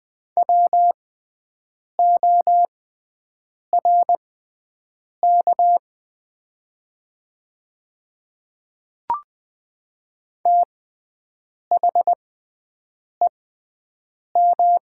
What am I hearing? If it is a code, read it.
Morse code, heard as a keyed tone, about 21 wpm, spaced as for 7 wpm, 700 Hz. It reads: WORK THEM
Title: 6x Character Spacing - Top 300 Words - 20wpm